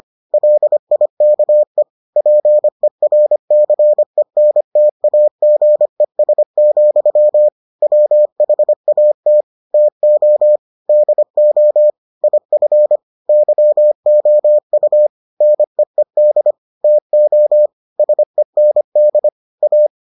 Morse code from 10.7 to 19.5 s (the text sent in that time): DO IF YOU NEED TO SEND